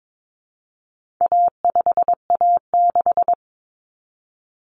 Morse code, heard as a keyed tone, about 22 wpm, 700 Hz. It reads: A5A6
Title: Common Cut Numbers - 4-Digit 22wpm